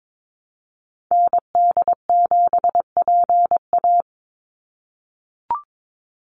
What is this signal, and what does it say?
Morse code, heard as a keyed tone, about 22 wpm, 700 Hz. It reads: ND7PA
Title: NA SOTA Activator Callsigns 22wpm